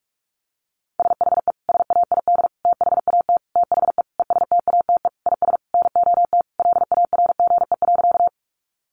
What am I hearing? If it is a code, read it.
Morse code, heard as a keyed tone, about 44 wpm, 700 Hz. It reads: SHE SAID THAT THE ESTATE IS NOT LARGE.
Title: Sentences from Top 1200 Words 45wpm